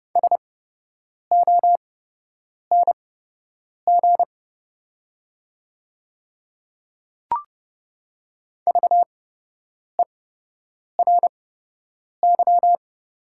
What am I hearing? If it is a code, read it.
Morse code, heard as a keyed tone, about 30 wpm, spaced as for 8 wpm, 700 Hz. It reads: SONG VERY
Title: 8x Character Spacing - Top 500 Words - 30wpm